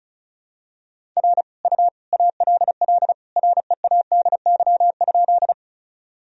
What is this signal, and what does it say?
Morse code, heard as a keyed tone, about 34 wpm, 700 Hz. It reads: R U ALL READY?
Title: Sentences from Top 400 Words and Abbreviations 35wpm